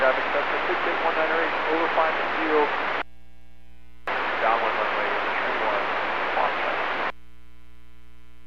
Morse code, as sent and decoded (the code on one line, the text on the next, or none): none